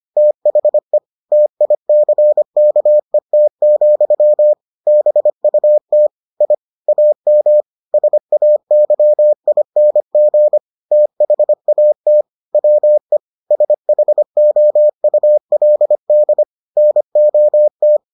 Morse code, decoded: THE TICKET, BUT I AM SAYING THAT WE SHOULD NOT

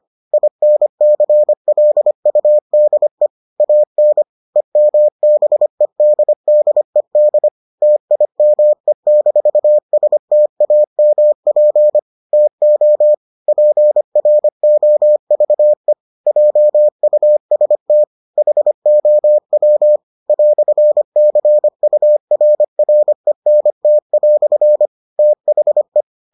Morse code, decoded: INCLUDE AN EMBEDDED TIME-STAMP TO PROVE JUST HOW "CURRENT" THE

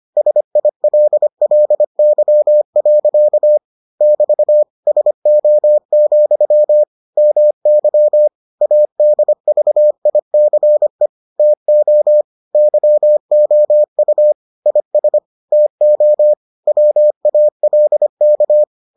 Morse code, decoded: SILLY. <BT> SO, MY ADVICE TO YOU IS TO WALK

